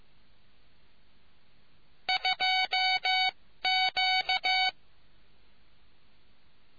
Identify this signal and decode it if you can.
Morse code, heard as a keyed tone, about 15 wpm, spaced as for 13 wpm, 750 Hz. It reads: 2Q